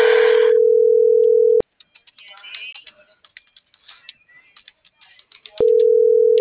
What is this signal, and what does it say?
Ringback tone (North America), 440 + 480 Hz.